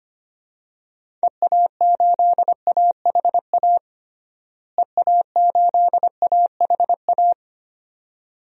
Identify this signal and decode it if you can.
Morse code, heard as a keyed tone, about 25 wpm, 700 Hz. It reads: EA8AHA EA8AHA
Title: DXCC-100 Call Signs - 6 Characters - Repeated 2x - 25wpm